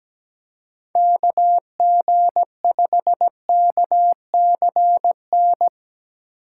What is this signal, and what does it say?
Morse code, heard as a keyed tone, about 19 wpm, 700 Hz. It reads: KG5KCN
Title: Speed Racing - USA 2x3 Call Signs 17wpm